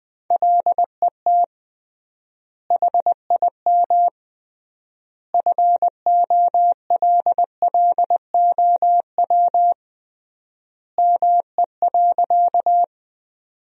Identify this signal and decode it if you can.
Morse code, heard as a keyed tone, about 20 wpm, 700 Hz. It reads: LET HIM FOLLOW ME.
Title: 3x Word-Spacing - Sentences from Top 300 Words - 20wpm